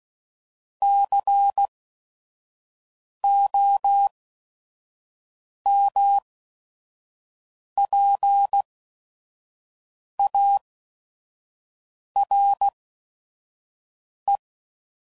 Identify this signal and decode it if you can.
Morse code, heard as a keyed tone, about 16 wpm, spaced as for 5 wpm, 800 Hz. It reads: COMPARE
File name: CW Code Groups @5WPM